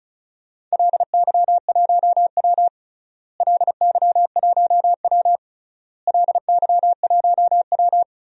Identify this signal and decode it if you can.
Morse code, heard as a keyed tone, about 36 wpm, 700 Hz. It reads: LY1W LY1W LY1W
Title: DXCC-100 Call Signs - 4 Characters - Repeated 3x - 35wpm